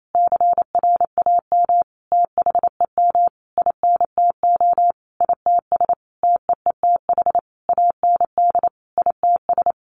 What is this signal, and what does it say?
Morse code, heard as a keyed tone, about 28 wpm, 700 Hz. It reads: CRAM THEM INTO ITS TEETH AND ITS
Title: A Sound Of Thunder 28wpm